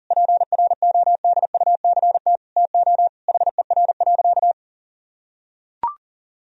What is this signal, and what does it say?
Morse code, heard as a keyed tone, about 40 wpm, 700 Hz. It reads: PRODUCT TO HER.
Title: Sentences from Top 1000 Words - Review Part 3 - 40wpm